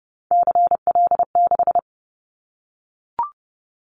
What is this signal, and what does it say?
Morse code, heard as a keyed tone, about 30 wpm, 700 Hz. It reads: CL6
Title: Callsign Prefixes for Top 100 DXCC - 30wpm